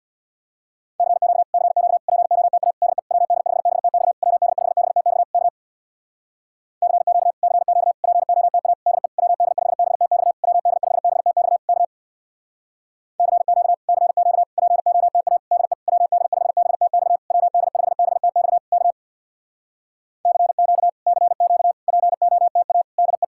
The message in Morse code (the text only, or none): CQ CQ POTA DE WD5BTX WD5BTX K CQ CQ POTA DE WD5BTX WD5BTX K CQ CQ POTA DE WD5BTX WD5BTX K CQ CQ POTA DE